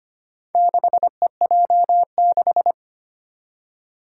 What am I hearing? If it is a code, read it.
Morse code, heard as a keyed tone, about 25 wpm, 700 Hz. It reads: 6EJ6
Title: Four Challenging Letter-Number Combinations 25wpm